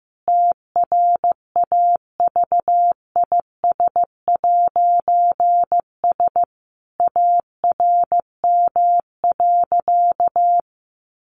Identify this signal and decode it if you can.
Morse code, heard as a keyed tone, about 15 wpm, 700 Hz. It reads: TRAVIS'S ARM.
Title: A Sound Of Thunder 15wpm